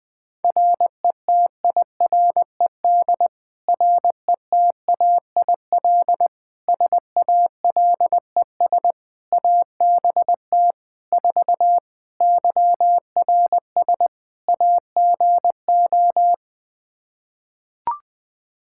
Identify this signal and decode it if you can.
Morse code, heard as a keyed tone, about 21 wpm, 700 Hz. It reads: RETIRED RETAIL SALES ABT 4 YRS AGO
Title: Practice QSO - Exchange 3 - More Info - 20wpm